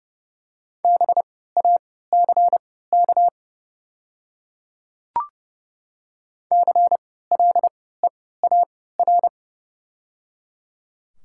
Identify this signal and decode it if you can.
Morse code, heard as a keyed tone, about 30 wpm, spaced as for 17 wpm, 700 Hz. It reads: BACK CLEAR
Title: 3x Character Spacing - Top 500 Words - 30wpm